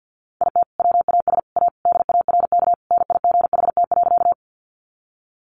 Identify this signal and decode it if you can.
Morse code, heard as a keyed tone, about 50 wpm, 700 Hz. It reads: IT WAS A DARK NIGHT.
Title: Sentences from Top 500 Words 50wpm